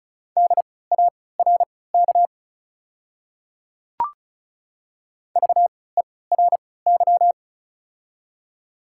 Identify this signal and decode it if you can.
Morse code, heard as a keyed tone, about 36 wpm, spaced as for 20 wpm, 700 Hz. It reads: DARK VERY
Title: 3x Character Spacing - Top 500 Words - 35wpm